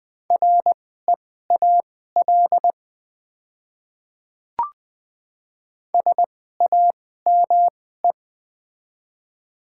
Morse code, heard as a keyed tone, about 20 wpm, spaced as for 15 wpm, 700 Hz. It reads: REAL SAME
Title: 2x Character Spacing - Top 300 Words - 20wpm